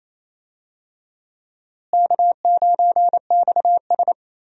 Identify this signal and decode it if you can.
Morse code, heard as a keyed tone, about 28 wpm, 700 Hz. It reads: K9XS
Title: USA Call Signs - 4 Characters - Repeated 3x - 28wpm